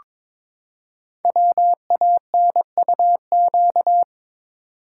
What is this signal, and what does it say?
Morse code, heard as a keyed tone, about 22 wpm, 700 Hz. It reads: WANUQ
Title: Code Groups - 5 Letter:Numbers - 22wpm